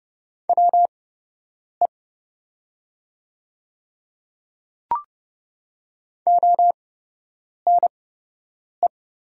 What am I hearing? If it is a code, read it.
Morse code, heard as a keyed tone, about 29 wpm, spaced as for 8 wpm, 700 Hz. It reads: WE ONE
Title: 8x Character Spacing - Top 100 Words - 30wpm